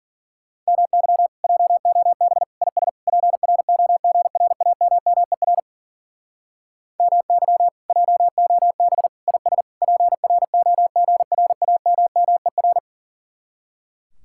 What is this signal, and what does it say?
Morse code, heard as a keyed tone, about 48 wpm, 700 Hz. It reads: MY JOB IS PROGRAMMER MY JOB IS PROGRAMMER
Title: Speed-Racing Common QSO Phrases Encore - 40wpm